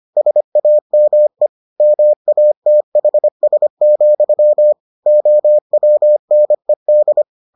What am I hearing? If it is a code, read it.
Morse code, heard as a keyed tone, about 25 wpm, 600 Hz. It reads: SAME MATHS, OWNED